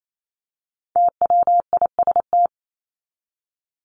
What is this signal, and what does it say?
Morse code, heard as a keyed tone, about 28 wpm, 700 Hz. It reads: TWIST
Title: Top 1751-2000 Words 28wpm